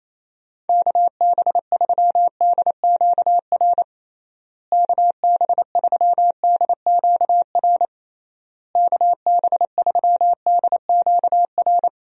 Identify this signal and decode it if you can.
Morse code, heard as a keyed tone, about 28 wpm, 700 Hz. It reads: KB3DQR KB3DQR KB3DQR
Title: USA Call Signs - 6 Characters - Repeated 3x - 28wpm